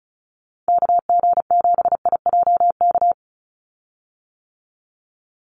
Morse code, heard as a keyed tone, about 34 wpm, 700 Hz. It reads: KG7IJK
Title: USA Call Signs 35wpm